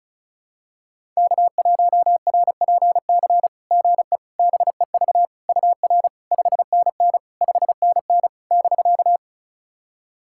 Morse code, heard as a keyed tone, about 34 wpm, 690 Hz. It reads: K1RPC GE BEV UR 5NN 5NN <BK>
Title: SOTA QSO Practice 35wpm